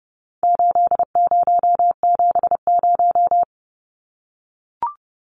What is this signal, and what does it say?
Morse code, heard as a keyed tone, about 30 wpm, 700 Hz. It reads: 8070
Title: Numbers - 4-Digit 30wpm